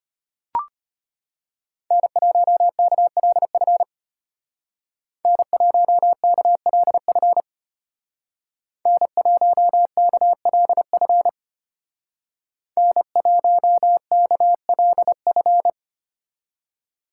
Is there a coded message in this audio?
Yes: Morse code from 1.9 to 15.7 s, heard as a keyed tone, about 33 wpm, 700 Hz. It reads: N1KLF N1KLF N1KLF N1KLF